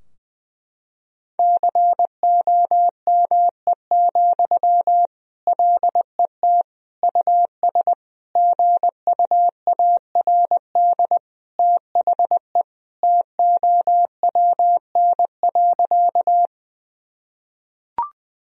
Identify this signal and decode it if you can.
Morse code, heard as a keyed tone, about 21 wpm, 700 Hz. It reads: COME, LET US GUARD THE TOWN.
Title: Sentences from Top 1000 Words - Review Part 4 - 20wpm